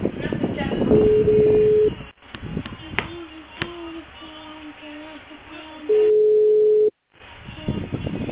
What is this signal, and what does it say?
Ringback tone (Europe, 425 Hz), with speech.